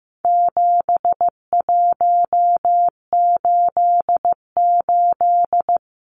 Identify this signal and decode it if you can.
Morse code, heard as a keyed tone, about 15 wpm, 700 Hz. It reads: 7188
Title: Numbers - 4-Digit 15wpm